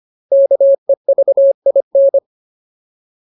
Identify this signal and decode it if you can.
Morse code, heard as a keyed tone, about 25 wpm, 550 Hz. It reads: KEVIN
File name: Ultimate Word List / 25WPM Ultimate